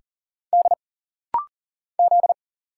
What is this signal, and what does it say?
Morse code, heard as a keyed tone, about 40 wpm, 700 Hz. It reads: D Z
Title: Single Letter-Number - Rapid-Fire 40wpm